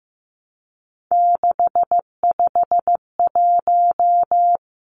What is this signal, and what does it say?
Morse code, heard as a keyed tone, about 15 wpm, 700 Hz. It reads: 651
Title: Numbers - 3-Digit 15wpm